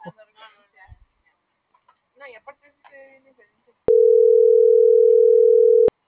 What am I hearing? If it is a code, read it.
Ringback tone (North America), 440 + 480 Hz.